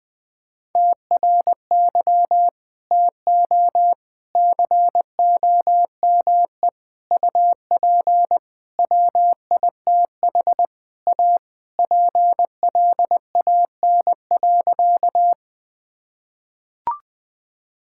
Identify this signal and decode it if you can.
Morse code, heard as a keyed tone, about 21 wpm, 700 Hz. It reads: TRY TO COME UP WITH A PLAN.
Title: Sentences with 4-Word Phrases - 20wpm